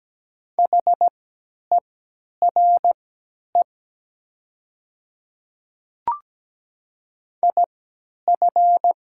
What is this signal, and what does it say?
Morse code, heard as a keyed tone, about 17 wpm, spaced as for 10 wpm, 700 Hz. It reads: HERE IF
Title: 3x Character Spacing - Top 200 Words - 17wpm